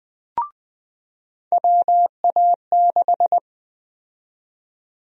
Morse code, heard as a keyed tone, about 20 wpm, 700 Hz. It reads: WA6
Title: USA Call Sign Prefixes - 20wpm